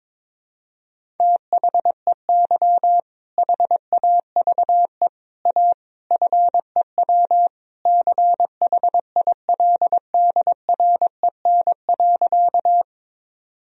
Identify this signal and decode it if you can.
Morse code, heard as a keyed tone, about 22 wpm, 700 Hz. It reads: THEY HAVE A FEW CHILDREN.
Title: Sentences from Top 1000 Words - Review Part 1 - 22wpm